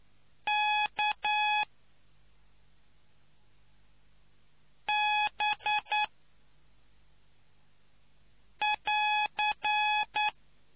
Morse code, heard as a keyed tone, about 9 wpm, 840 Hz. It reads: K B <AR>